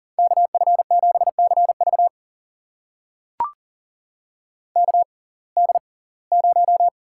Morse code, heard as a keyed tone, about 40 wpm, spaced as for 24 wpm, 700 Hz. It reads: KF7CV KD0
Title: USA Call Signs - 5 Characters - 6x Character Spacing - 40wpm